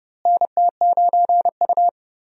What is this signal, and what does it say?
Morse code, heard as a keyed tone, about 31 wpm, 700 Hz. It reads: NT9U
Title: USA Call Signs - 4 Characters - Repeated 3x - 30wpm